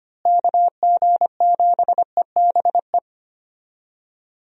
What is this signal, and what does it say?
Morse code, heard as a keyed tone, about 25 wpm, 700 Hz. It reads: KG7EBE
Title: USA Call Signs - 6 Characters - Repeated 3x - 25wpm